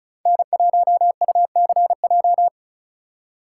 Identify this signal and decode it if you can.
Morse code, heard as a keyed tone, about 36 wpm, 700 Hz. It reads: N1UCJ
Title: USA Call Signs - 5 Characters - 2x Character Spacing - 35wpm